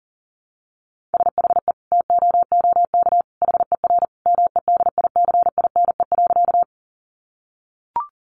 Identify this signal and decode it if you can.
Morse code, heard as a keyed tone, about 40 wpm, 700 Hz. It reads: SHE TOOK HER MEDICINE.
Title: Sentences from Top 900 Words 40wpm